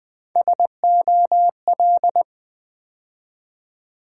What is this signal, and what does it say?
Morse code, heard as a keyed tone, about 20 wpm, 700 Hz. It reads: SOL
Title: Build A Word - 3 Letters At A Time - 8-Letter Words - 20wpm